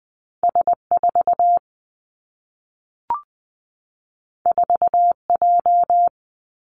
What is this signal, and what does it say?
Morse code, heard as a keyed tone, about 20 wpm, 700 Hz. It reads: S 4 4 J
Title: Two Challenging Letter-Number Combinations 20wpm